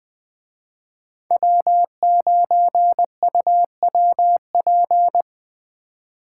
Morse code, heard as a keyed tone, about 19 wpm, 700 Hz. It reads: W9UWP
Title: Speed Racing - USA 1x3 Call Signs 17wpm